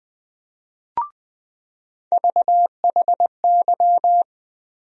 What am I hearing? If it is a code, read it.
Morse code, heard as a keyed tone, about 21 wpm, 700 Hz. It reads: VHY
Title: USA Call Sign Postfixes - 3 Letters 20wpm